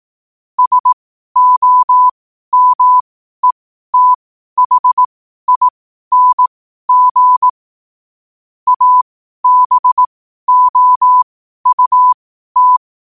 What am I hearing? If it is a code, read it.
Morse code, heard as a keyed tone, about 18 wpm, spaced as for 12 wpm, 1000 Hz. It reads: SOMETHING ABOUT
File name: qotdfiles / 12wpm